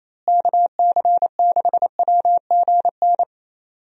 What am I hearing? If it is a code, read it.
Morse code, heard as a keyed tone, about 27 wpm, 700 Hz. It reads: KC6WGN
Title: USA Call Signs - 6 Characters - Repeated 2x - 28wpm